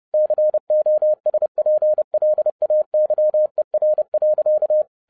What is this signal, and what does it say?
Morse code, heard as a keyed tone, about 30 wpm, 610 Hz. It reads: COSPLAYER.